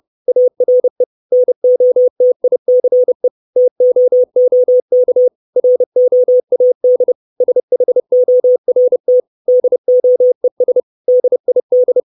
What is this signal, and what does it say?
Morse code, heard as a keyed tone, about 30 wpm, 500 Hz. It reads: ARE NOTICE TOOK ROAD SHORT DOES DID
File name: 632 Words 30WPM